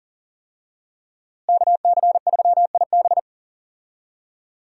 Morse code, heard as a keyed tone, about 40 wpm, 700 Hz. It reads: KC3IB
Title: USA Call Signs - 5 Characters - Repeated 3x - 40wpm